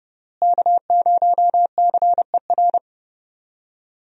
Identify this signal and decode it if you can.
Morse code, heard as a keyed tone, about 32 wpm, 700 Hz. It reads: K0CER
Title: USA Call Signs - 5 Characters - Repeated 2x - 30wpm